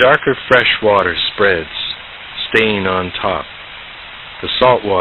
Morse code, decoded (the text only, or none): none